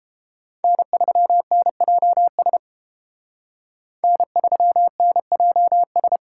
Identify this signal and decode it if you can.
Morse code, heard as a keyed tone, about 31 wpm, 700 Hz. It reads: N3NJS N3NJS